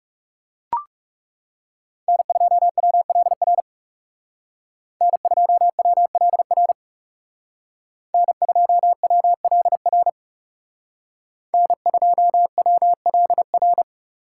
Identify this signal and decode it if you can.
Morse code, heard as a keyed tone, about 34 wpm, 700 Hz. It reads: N2WLR N2WLR N2WLR N2WLR
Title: Speed Racing - USA 1x3 Call Signs 30wpm